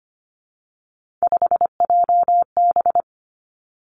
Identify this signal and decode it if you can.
Morse code, heard as a keyed tone, about 25 wpm, 700 Hz. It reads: 5JB
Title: Three Challenging Letter-Number Combinations 25wpm